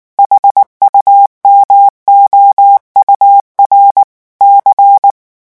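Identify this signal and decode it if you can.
Morse code, heard as a keyed tone, about 19 wpm, 800 Hz. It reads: HUMOUR C
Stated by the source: French Morse revision 8 drill of random five-letter groups at 18 words per minute.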